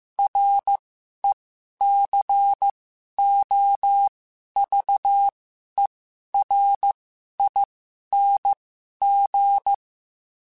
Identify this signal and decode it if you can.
Morse code, heard as a keyed tone, about 15 wpm, spaced as for 11 wpm, 800 Hz. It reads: RECOVERING
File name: Special INGs-mp3s / Special INGs-mp3s / Sngl Ing @10